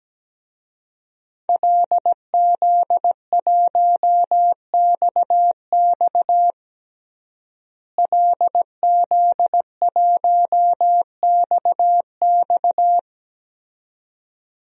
Morse code, heard as a keyed tone, about 17 wpm, 700 Hz. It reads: LZ1XX LZ1XX
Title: DXCC-100 Call Signs - 5 Characters - Repeated 2x - 17wpm